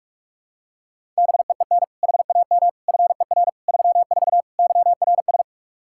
Morse code, heard as a keyed tone, about 42 wpm, 700 Hz. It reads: BEEN HAM FER 34 YRS
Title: Practice QSO - Exchange 3 - More Info - 45wpm